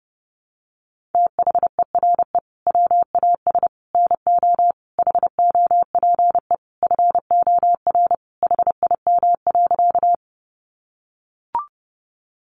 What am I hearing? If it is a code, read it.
Morse code, heard as a keyed tone, about 30 wpm, 700 Hz. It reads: THERE WAS NO HOPE FOR HIM.
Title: Sentences from Top 1000 Words 30wpm